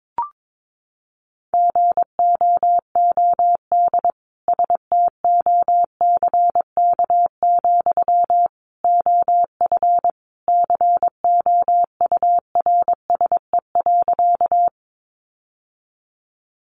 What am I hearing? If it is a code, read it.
Morse code, heard as a keyed tone, about 22 wpm, 700 Hz. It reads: GOOD STOCK, OF COURSE.
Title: Sentences from Top 800 Words 22wpm